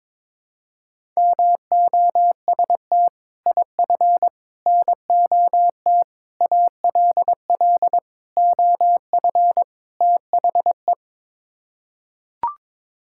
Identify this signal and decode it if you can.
Morse code, heard as a keyed tone, about 22 wpm, 700 Hz. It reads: MOST IF NOT ALL OF THE
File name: practice / 22wpm / Sets of 6 Words - 22wpm